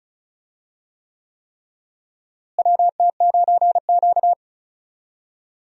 Morse code, heard as a keyed tone, about 36 wpm, 700 Hz. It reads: WT9Q
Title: USA Call Signs - 4 Characters - Repeated 3x - 35wpm